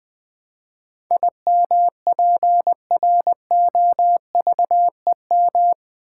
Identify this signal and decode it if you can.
Morse code, heard as a keyed tone, about 20 wpm, 700 Hz. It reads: IMPROVEM
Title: Build A Word - 4 Letters At A Time - 11-Letter Words - 20wpm